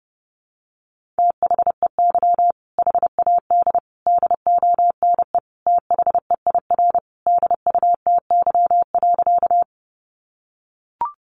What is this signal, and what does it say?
Morse code, heard as a keyed tone, about 30 wpm, 700 Hz. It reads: THEY HAD DONE THEIR DUTY.
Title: Sentences from Top 1500 Words 30wpm